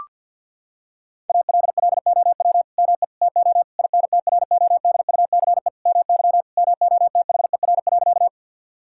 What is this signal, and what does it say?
Morse code, heard as a keyed tone, about 50 wpm, 700 Hz. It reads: ALLOW ME TO INTRODUCE MY MOTHER.
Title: Sentences from Top 1600 Words 50wpm